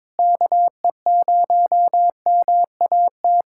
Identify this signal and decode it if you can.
Morse code, heard as a keyed tone, about 21 wpm, 700 Hz. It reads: KE0MAT
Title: USA Callsign DE Callsign - 6 Characters - 22wpm